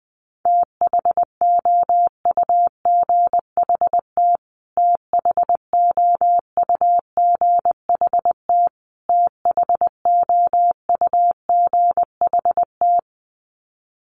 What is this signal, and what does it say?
Morse code, heard as a keyed tone, about 20 wpm, 700 Hz. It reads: THOUGHT THOUGHT THOUGHT
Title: Top 300 Words Repeated 20wpm